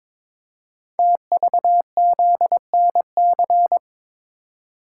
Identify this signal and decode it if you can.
Morse code, heard as a keyed tone, about 22 wpm, 690 Hz. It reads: TVZNC
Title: Code Groups - 5 Letter:Numbers - 22wpm